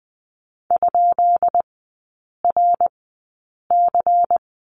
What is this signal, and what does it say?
Morse code, heard as a keyed tone, about 20 wpm, spaced as for 8 wpm, 700 Hz. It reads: ?RC
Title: 133 Lesson 22e - Question Mark - Three Characters 20wpm